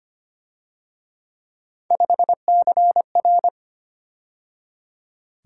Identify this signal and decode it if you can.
Morse code, heard as a keyed tone, about 31 wpm, 700 Hz. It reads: 5CR
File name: Speed-Racing - Three Letter-Number Combinations - 25wpm